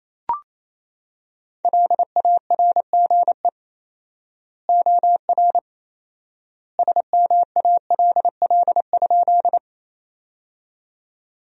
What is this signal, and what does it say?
Morse code, heard as a keyed tone, about 28 wpm, 700 Hz. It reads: LARGE OR SMALL?
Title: 4x Word-Spacing - Sentences from Top 200 Words - 28wpm